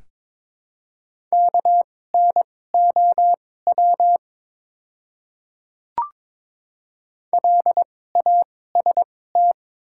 Morse code, heard as a keyed tone, about 22 wpm, spaced as for 16 wpm, 700 Hz. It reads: KNOW LAST
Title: 2x Character Spacing - Top 300 Words - 22wpm